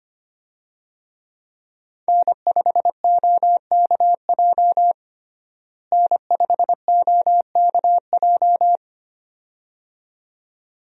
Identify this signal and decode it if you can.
Morse code, heard as a keyed tone, about 25 wpm, 700 Hz. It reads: N5OKJ N5OKJ
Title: USA Call Signs - 5 Characters - Repeated 2x - 25wpm